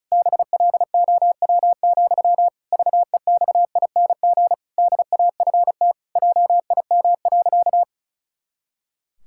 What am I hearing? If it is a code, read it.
Morse code, heard as a keyed tone, about 36 wpm, 700 Hz. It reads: BLOW, VEXING DAFT JIM.
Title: Pangram Challenge v2 - 35wpm